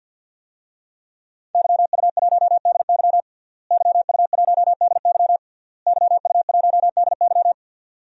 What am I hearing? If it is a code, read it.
Morse code, heard as a keyed tone, about 48 wpm, 700 Hz. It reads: YU1DY YU1DY YU1DY
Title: DXCC-100 Call Signs - 5 Characters - Repeated 3x - 50wpm